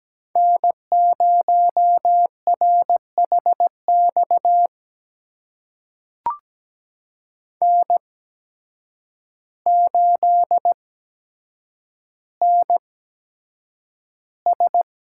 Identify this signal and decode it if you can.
Morse code, heard as a keyed tone, about 17 wpm, spaced as for 8 wpm, 700 Hz. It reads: N0RHX N8NS
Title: USA Call Signs - 5 Characters - 8x Character Spacing - 17wpm